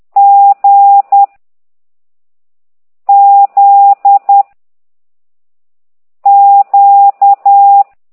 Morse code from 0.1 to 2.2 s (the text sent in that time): G